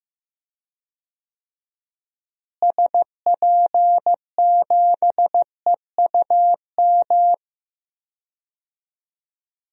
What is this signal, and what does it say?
Morse code, heard as a keyed tone, about 15 wpm, 700 Hz. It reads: SP7EUM
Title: DXCC-100 Call Signs - 6 Characters - Repeated 2x - 15wpm